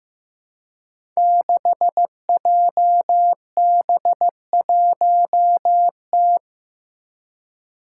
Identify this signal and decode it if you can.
Morse code, heard as a keyed tone, about 15 wpm, 700 Hz. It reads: 6JB1T
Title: Five Challenging Letter-Number Combinations 15wpm